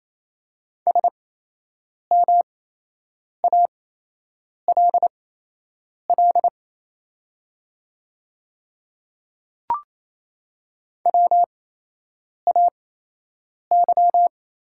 Morse code, heard as a keyed tone, about 27 wpm, spaced as for 8 wpm, 700 Hz. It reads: SMALL WAY
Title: 8x Character Spacing - Top 200 Words - 28wpm